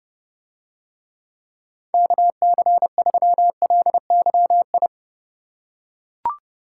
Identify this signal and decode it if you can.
Morse code, heard as a keyed tone, about 30 wpm, 700 Hz. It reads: KC3LYI